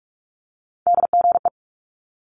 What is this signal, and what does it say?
Morse code, heard as a keyed tone, about 46 wpm, 700 Hz. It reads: DGE
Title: Multi-letter Phonemes Quick Fire 45wpm